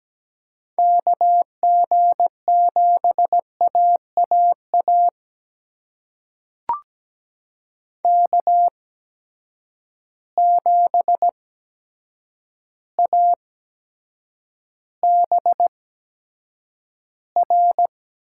Morse code, heard as a keyed tone, about 17 wpm, spaced as for 8 wpm, 700 Hz. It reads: KG7AAA K7ABR
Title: USA Call Signs - 8x Character Spacing - 17wpm